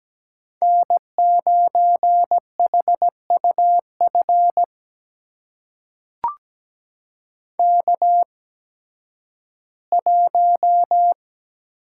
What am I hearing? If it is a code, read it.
Morse code, heard as a keyed tone, about 17 wpm, spaced as for 11 wpm, 700 Hz. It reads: N9HUF K1
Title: USA Call Signs - 5 Characters - 8x Character Spacing - 17wpm